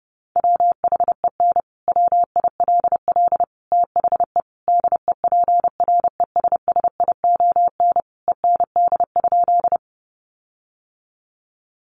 Morse code, heard as a keyed tone, about 28 wpm, 700 Hz. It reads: WHEN WILL THE DEPRESSION END?
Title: Sentences from Top 1300 Words 30wpm